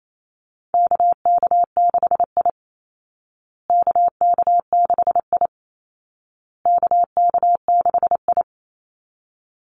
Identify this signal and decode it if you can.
Morse code, heard as a keyed tone, about 28 wpm, 700 Hz. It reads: KK6I KK6I KK6I
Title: Call Signs Repeated 2x - 28wpm